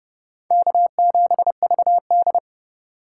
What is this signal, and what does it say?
Morse code, heard as a keyed tone, about 31 wpm, 700 Hz. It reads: K7VD
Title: USA Call Signs - 2x Character Spacing - 30wpm